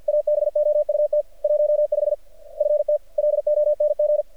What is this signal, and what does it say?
Morse code, heard as a keyed tone, about 53 wpm, 600 Hz. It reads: ABOUT 15 FT LONG
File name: archived / CRSnet